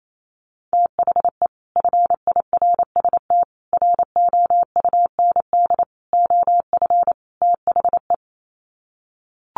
Morse code, heard as a keyed tone, about 28 wpm, 700 Hz. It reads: THE FIRST ROUND OF THE
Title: Sets of 5 Words Encore 28wpm